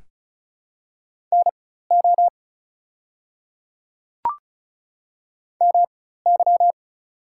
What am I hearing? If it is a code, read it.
Morse code, heard as a keyed tone, about 36 wpm, spaced as for 16 wpm, 700 Hz. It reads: NO MY
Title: 4x Character Spacing - Top 300 Words - 35wpm